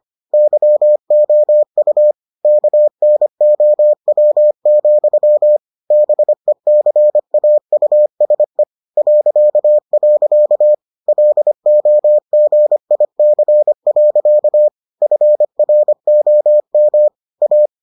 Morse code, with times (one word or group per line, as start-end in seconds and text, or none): 0.3-2.1 YOU
2.4-5.6 KNOW,
5.9-8.6 BECAUSE
9.0-10.7 ..
11.1-14.7 LOGIC.
15.0-17.1 FROM
17.4-17.7 A